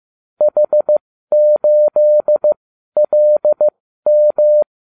Morse code, heard as a keyed tone, about 15 wpm, spaced as for 12 wpm, 600 Hz. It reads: H8LM